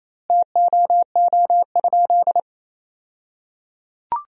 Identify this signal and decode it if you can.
Morse code, heard as a keyed tone, about 27 wpm, 700 Hz. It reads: TOO?
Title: 2x Word-Spacing - Sentences from Top 200 Words - 28wpm